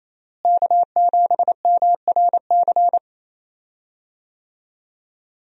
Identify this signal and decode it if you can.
Morse code, heard as a keyed tone, about 28 wpm, 700 Hz. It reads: K7MRC
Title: USA Call Signs - 5 Characters - Repeated 2x - 28wpm